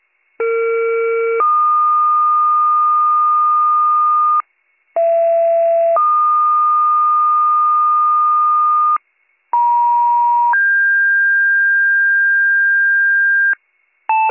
Two-tone page, heard as a one-second tone, then a three-second tone, about 460 then 1220 Hz, about 670 then 1220 Hz, about 950 then 1600 Hz. It is heard from 0.4 to 13.5 s.